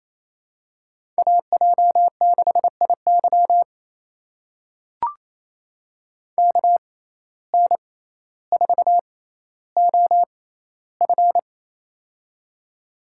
Morse code, heard as a keyed tone, about 28 wpm, spaced as for 14 wpm, 700 Hz. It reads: AJ6IY KN4OF
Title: USA Call Signs - 5 Characters - 6x Character Spacing - 28wpm